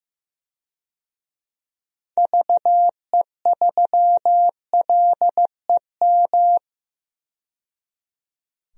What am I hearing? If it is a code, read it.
Morse code, heard as a keyed tone, about 15 wpm, 700 Hz. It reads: VE3LEM
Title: DXCC-100 Call Signs - 6 Characters - 15wpm